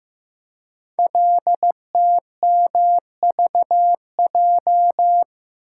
Morse code, heard as a keyed tone, about 15 wpm, 700 Hz. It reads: LTMVJ
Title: Code Groups - 5 Letters - 15wpm